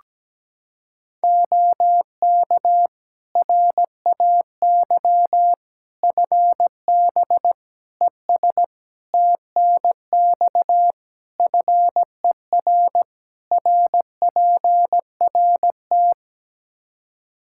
Morse code, heard as a keyed tone, about 17 wpm, 700 Hz. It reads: OK RAY FB ES TNX FER RPRT
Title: Practice QSO - Exchange 2 - Info - 17wpm